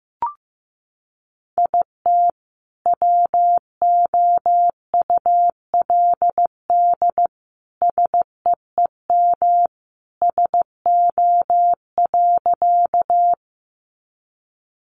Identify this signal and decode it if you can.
Morse code, heard as a keyed tone, about 15 wpm, 700 Hz. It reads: IT WOULD SEEM SO.
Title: Sentences from Top 300 Words 15wpm